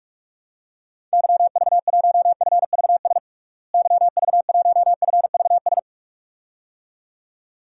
Morse code, heard as a keyed tone, about 48 wpm, 700 Hz. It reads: YV1FVS YV1FVS
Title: DXCC-100 Call Signs - 6 Characters - Repeated 2x - 45wpm